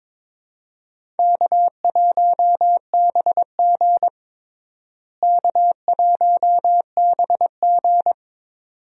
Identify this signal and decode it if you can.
Morse code, heard as a keyed tone, about 22 wpm, 700 Hz. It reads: K1BG K1BG
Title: USA Call Signs - 4 Characters - Repeated 2x - 22wpm